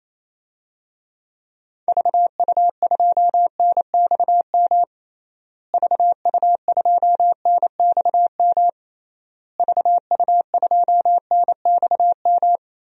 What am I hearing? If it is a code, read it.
Morse code, heard as a keyed tone, about 28 wpm, 700 Hz. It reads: VU2NXM VU2NXM VU2NXM
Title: DXCC-100 Call Signs - 6 Characters - Repeated 3x - 28wpm